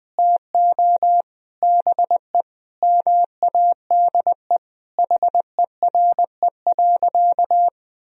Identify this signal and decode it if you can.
Morse code, heard as a keyed tone, about 20 wpm, 700 Hz. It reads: TO BE MADE HERE.
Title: Sentences from Top 1000 Words - Review Part 4 - 20wpm